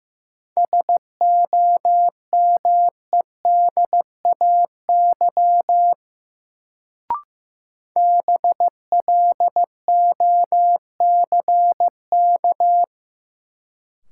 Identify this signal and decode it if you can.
Morse code, heard as a keyed tone, about 15 wpm, 700 Hz. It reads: SOMEDAY BLOCK
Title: Sentences from Top 1600 Words - New Word Drill Alt - 15wpm